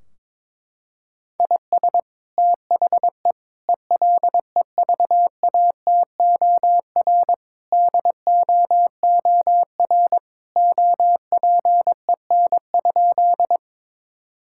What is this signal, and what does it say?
Morse code, heard as a keyed tone, about 22 wpm, 700 Hz. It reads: IS THE ELEVATOR DOOR OPEN?
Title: Sentences from Top 1600 Words 22wpm